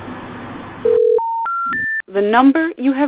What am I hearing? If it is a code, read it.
Special information tone, with speech.